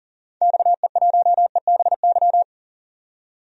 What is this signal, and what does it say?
Morse code, heard as a keyed tone, about 40 wpm, 700 Hz. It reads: XE1EBY
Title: DXCC-100 Call Signs - 6 Characters - 40wpm